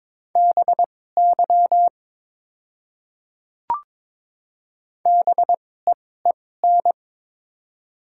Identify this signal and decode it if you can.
Morse code, heard as a keyed tone, about 22 wpm, spaced as for 16 wpm, 700 Hz. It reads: BY BEEN
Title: 2x Character Spacing - Top 500 Words - 22wpm